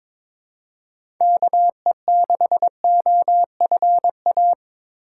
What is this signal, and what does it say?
Morse code, heard as a keyed tone, about 22 wpm, 700 Hz. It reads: KE6OFA
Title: USA Call Signs - 6 Characters - 3x Character Spacing - 22wpm